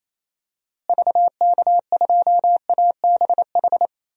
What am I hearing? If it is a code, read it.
Morse code, heard as a keyed tone, about 28 wpm, 700 Hz. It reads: VK2ABH
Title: DXCC-100 Call Signs - 6 Characters - Repeated 2x - 28wpm